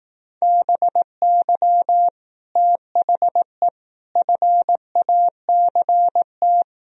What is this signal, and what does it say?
Morse code, heard as a keyed tone, about 18 wpm, 700 Hz. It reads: BY THE FACT
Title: Speed-Racing - Sets of 4 Words 15wpm